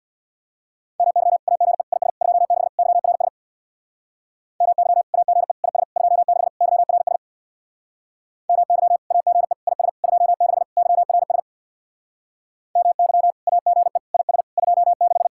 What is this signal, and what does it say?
Morse code, heard as a keyed tone, about 66 wpm, 700 Hz. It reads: MY AGE IS 26 YRS MY AGE IS 26 YRS MY AGE IS 26 YRS MY AGE IS 26